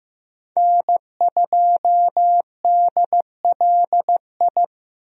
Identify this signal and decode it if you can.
Morse code, heard as a keyed tone, about 15 wpm, 700 Hz. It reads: N2DLI
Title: USA Call Signs - 5 Characters - Repeated 2x - 15wpm